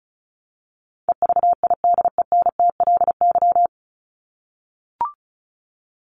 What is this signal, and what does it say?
Morse code, heard as a keyed tone, about 34 wpm, 700 Hz. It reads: EVIDENTLY
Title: LY Suffix - Top 100 Words with Base Words 35wpm